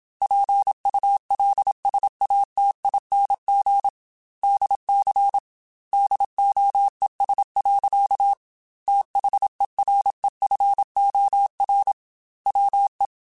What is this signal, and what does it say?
Morse code, heard as a keyed tone, about 27 wpm, 800 Hz. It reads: PULSATING DC DOES. THEREFOR WE
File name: Technical Subjects / Technical Subjects-mp3s / Sw Pwr Sup @25